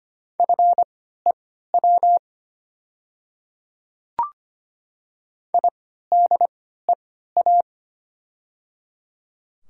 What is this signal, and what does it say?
Morse code, heard as a keyed tone, about 25 wpm, spaced as for 14 wpm, 700 Hz. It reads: FEW IDEA
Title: 3x Character Spacing - Top 500 Words - 25wpm